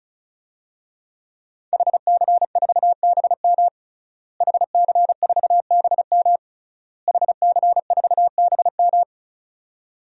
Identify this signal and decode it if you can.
Morse code, heard as a keyed tone, about 34 wpm, 700 Hz. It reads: HC4BM HC4BM HC4BM
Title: DXCC-100 Call Signs - 5 Characters - Repeated 3x - 35wpm